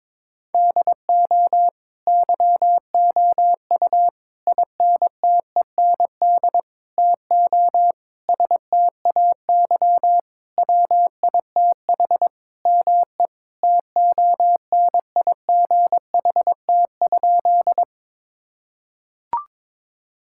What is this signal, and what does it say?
Morse code, heard as a keyed tone, about 21 wpm, 700 Hz. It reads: DO YOU INTEND TO STAY WITH ME TONIGHT?
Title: Sentences from Top 1600 Words 22wpm